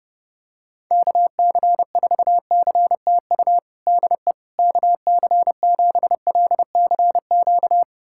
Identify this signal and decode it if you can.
Morse code, heard as a keyed tone, about 32 wpm, 700 Hz. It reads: KC4CTU DE KC7LCQ
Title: USA Callsign DE Callsign - 6 Characters - 30wpm